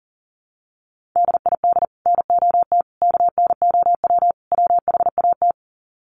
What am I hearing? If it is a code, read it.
Morse code, heard as a keyed tone, about 40 wpm, 700 Hz. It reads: DID NOT KNOW WHAT
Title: Sets of 4 Words 40wpm